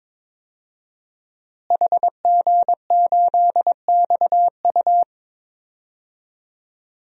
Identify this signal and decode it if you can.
Morse code, heard as a keyed tone, about 22 wpm, 700 Hz. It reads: HG8XU
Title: DXCC-100 Call Signs - 5 Characters - Repeated 3x - 22wpm